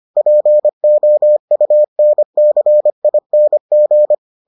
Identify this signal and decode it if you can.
Morse code, heard as a keyed tone, about 25 wpm, 600 Hz. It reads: POUNCING